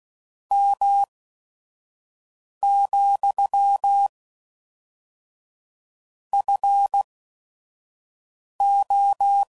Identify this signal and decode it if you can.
Morse code, heard as a keyed tone, about 16 wpm, spaced as for 5 wpm, 800 Hz. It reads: M, FO
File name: CW Code Groups @5WPM